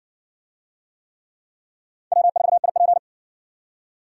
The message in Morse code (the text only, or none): W4IP